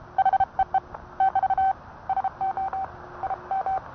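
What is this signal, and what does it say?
Morse code, heard as a keyed tone, about 30 wpm, 770 Hz. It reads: HEE <BT> SO IM
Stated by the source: Morse code sent by N1EA on a Vibroplex key.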